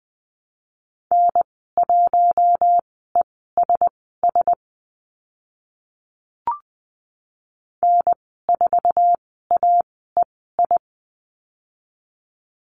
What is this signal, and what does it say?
Morse code, heard as a keyed tone, about 20 wpm, spaced as for 14 wpm, 700 Hz. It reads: N1ESS N4AEI
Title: 048 Lesson 9g - Number 4 - Call Signs 20wpm